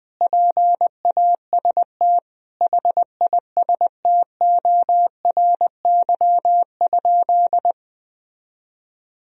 Morse code, heard as a keyed tone, about 20 wpm, 700 Hz. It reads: PAST HISTORY?
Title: Sentences from Top 1000 Words - Review Part 3 - 20wpm